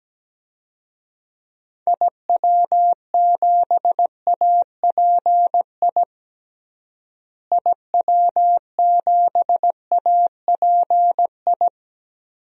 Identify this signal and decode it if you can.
Morse code, heard as a keyed tone, about 17 wpm, 700 Hz. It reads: IW7API IW7API
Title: DXCC-100 Call Signs - 6 Characters - Repeated 3x - 17wpm